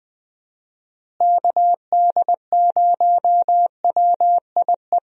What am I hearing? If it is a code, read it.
Morse code, heard as a keyed tone, about 20 wpm, 700 Hz. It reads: KD0WIE
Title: USA Call Signs - 6 Characters - Repeated 2x - 20wpm